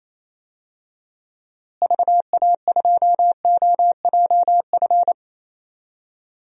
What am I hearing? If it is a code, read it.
Morse code, heard as a keyed tone, about 28 wpm, 700 Hz. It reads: VA2OJF
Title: DXCC-100 Call Signs - 6 Characters - Repeated 3x - 28wpm